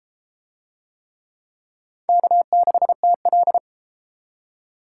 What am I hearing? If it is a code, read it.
Morse code, heard as a keyed tone, about 30 wpm, 700 Hz. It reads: K6TL